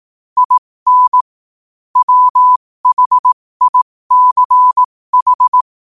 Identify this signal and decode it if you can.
Morse code, heard as a keyed tone, about 18 wpm, spaced as for 15 wpm, 1000 Hz. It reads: IN WHICH
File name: qotdfiles / 15wpm